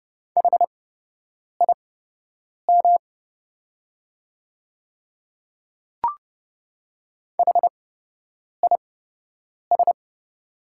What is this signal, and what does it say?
Morse code, heard as a keyed tone, about 31 wpm, spaced as for 8 wpm, 700 Hz. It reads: HIM HIS
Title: 8x Character Spacing - Top 100 Words - 30wpm